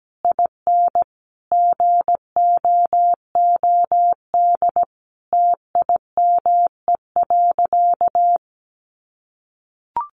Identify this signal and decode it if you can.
Morse code, heard as a keyed tone, about 17 wpm, 700 Hz. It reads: IN GOOD TIME.
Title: Sentences from Top 1200 Words 17wpm